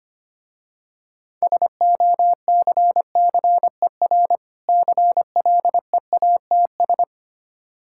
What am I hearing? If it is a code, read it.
Morse code, heard as a keyed tone, about 25 wpm, 700 Hz. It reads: SOCCER CLEATS
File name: practice / 25wpm / Difficult Words with Hints - 25wpm